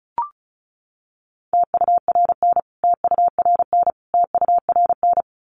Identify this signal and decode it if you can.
Morse code, heard as a keyed tone, about 34 wpm, 700 Hz. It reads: TURN TURN TURN
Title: Top 200 Words Repeated 35wpm